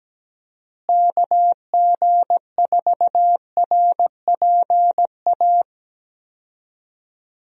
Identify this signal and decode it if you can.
Morse code, heard as a keyed tone, about 17 wpm, 700 Hz. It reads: KG4RPA
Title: USA Call Signs - 6 Characters - Repeated 3x - 17wpm